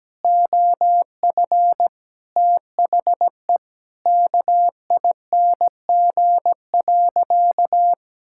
Morse code, heard as a keyed tone, about 17 wpm, 700 Hz. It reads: OF THE KING.
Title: Sentences from Top 1000 Words - Review Part 6 - 17wpm